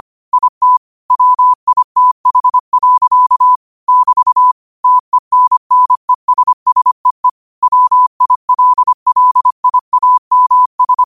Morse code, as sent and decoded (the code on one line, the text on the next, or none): .. - / .-- .. - .... .-.-.- / -...- / - . -. -. ... ... . / .-- .. .-.. .-.. .. .- -- ...
IT WITH. <BT> TENNESSEE WILLIAMS